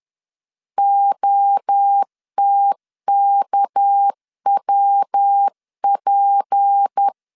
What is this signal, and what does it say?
Morse code, heard as a keyed tone, about 10 wpm, 800 Hz. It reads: OTKWP